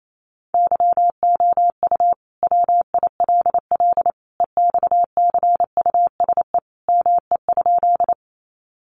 Morse code, heard as a keyed tone, about 28 wpm, 700 Hz. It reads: YOU WILL EXCUSE ME?
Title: Sentences from Top 700 Words 28wpm